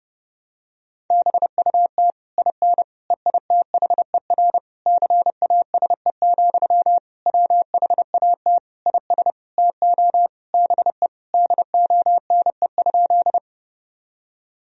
Morse code, heard as a keyed tone, about 32 wpm, 700 Hz. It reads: BUT IN EITHER CASE, WHAT IS TO BE DONE?
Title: Sentences from Top 1000 Words - Review Part 4 - 30wpm